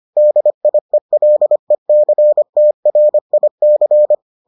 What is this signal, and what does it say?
Morse code, heard as a keyed tone, about 25 wpm, 600 Hz. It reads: DIELECTRIC